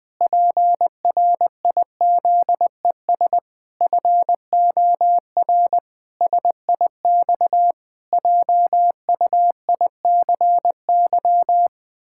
Morse code, heard as a keyed tone, about 20 wpm, 700 Hz. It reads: PRIZES FOR SIX JUICY
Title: Pangram Challenge v2 - 20wpm